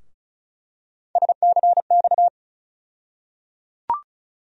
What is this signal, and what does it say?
Morse code, heard as a keyed tone, about 36 wpm, 700 Hz. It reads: SCX
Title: USA Call Sign Postfixes - 3 Letters 35wpm